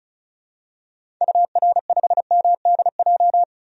Morse code, heard as a keyed tone, about 34 wpm, 700 Hz. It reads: UR5MDJ
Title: DXCC-100 Call Signs - 6 Characters - Repeated 3x - 35wpm